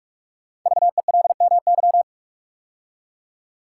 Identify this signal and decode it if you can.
Morse code, heard as a keyed tone, about 44 wpm, 700 Hz. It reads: VELMY
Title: Code Groups - 5 Letters - 45wpm